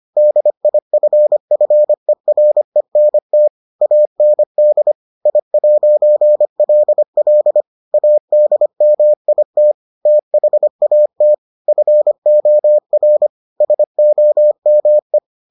Morse code, heard as a keyed tone, about 25 wpm, 600 Hz. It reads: DIFFERENT AND I'LL ADMIT THAT FOR SOME